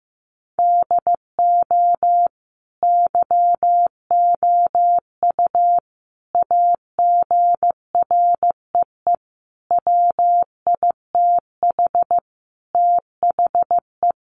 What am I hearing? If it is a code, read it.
Morse code, heard as a keyed tone, about 15 wpm, 700 Hz. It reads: DO YOU AGREE WITH THE
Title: Sentences from Top 1400 Words 15wpm